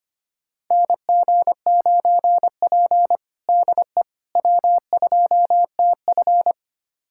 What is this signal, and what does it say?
Morse code, heard as a keyed tone, about 25 wpm, 700 Hz. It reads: NG9P DE W2TF
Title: USA Callsign DE Callsign - 4 Characters - 25wpm